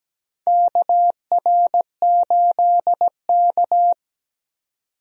Morse code, heard as a keyed tone, about 17 wpm, 700 Hz. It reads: KR8K
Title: USA Call Signs - 4 Characters - 17wpm